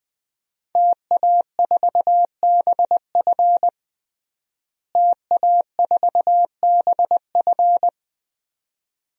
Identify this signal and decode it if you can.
Morse code, heard as a keyed tone, about 20 wpm, 700 Hz. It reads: TA4BF TA4BF
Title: DXCC-100 Call Signs - 5 Characters - Repeated 2x - 20wpm